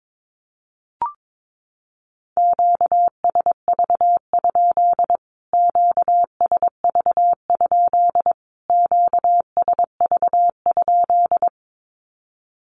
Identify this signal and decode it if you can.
Morse code, heard as a keyed tone, about 22 wpm, 700 Hz. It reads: QSV? QSV? QSV?
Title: Q-Codes Repeated 3x 22wpm